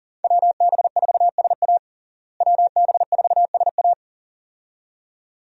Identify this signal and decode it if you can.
Morse code, heard as a keyed tone, about 40 wpm, 700 Hz. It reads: WB4SA WB4SA
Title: USA Call Signs - 5 Characters - Repeated 2x - 40wpm